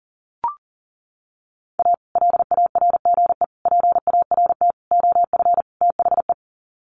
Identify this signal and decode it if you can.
Morse code, heard as a keyed tone, about 37 wpm, 700 Hz. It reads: A LARGE PART OF THE
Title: Sets of 5 Words Encore2 40wpm